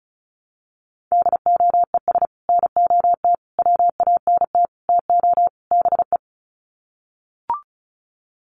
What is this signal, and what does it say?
Morse code, heard as a keyed tone, about 34 wpm, 700 Hz. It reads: DOES NOT WANT TO BE
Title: Sets of 5 Words Encore 35wpm